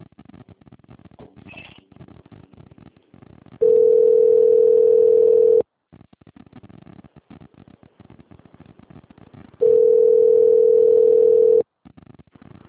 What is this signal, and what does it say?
Ringback tone (North America), 440 + 480 Hz.